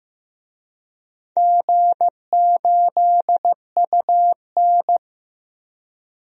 Morse code, heard as a keyed tone, about 15 wpm, 700 Hz. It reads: G8UN
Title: DXCC-100 Call Signs - 4 Characters - 15wpm